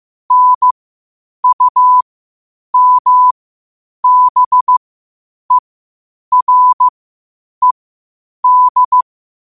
Morse code, heard as a keyed tone, about 15 wpm, spaced as for 8 wpm, 1000 Hz. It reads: NUMBERED
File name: qotdfiles / 8wpm